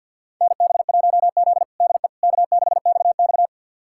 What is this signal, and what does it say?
Morse code, heard as a keyed tone, about 47 wpm, 690 Hz. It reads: NB1Z DE K6XX
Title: USA Callsign DE Callsign - 4 Characters - 50wpm